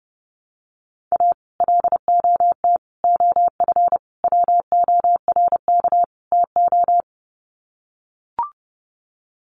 Morse code, heard as a keyed tone, about 30 wpm, 700 Hz. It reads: A LOT OF WORK TO
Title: Sets of 5 Words Encore2 30wpm